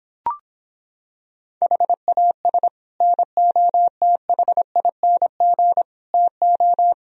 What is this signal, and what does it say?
Morse code, heard as a keyed tone, about 26 wpm, 700 Hz. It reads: HAS NOTHING TO